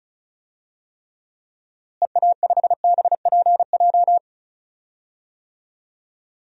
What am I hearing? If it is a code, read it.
Morse code, heard as a keyed tone, about 36 wpm, 700 Hz. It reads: EA5BPJ
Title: DXCC-100 Call Signs - 6 Characters - Repeated 3x - 35wpm